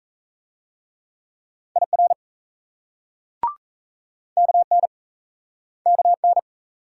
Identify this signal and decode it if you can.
Morse code, heard as a keyed tone, about 38 wpm, 700 Hz. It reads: IR KN KN